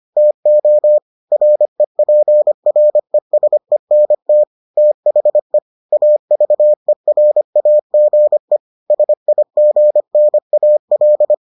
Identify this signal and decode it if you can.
Morse code, heard as a keyed tone, about 25 wpm, 600 Hz. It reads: TO REPRESENT THE AVERAGE SIGNAL